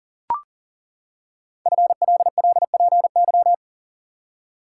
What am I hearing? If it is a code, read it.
Morse code, heard as a keyed tone, about 38 wpm, 690 Hz. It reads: FLLPY
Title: Code Groups - 5 Letters - 40wpm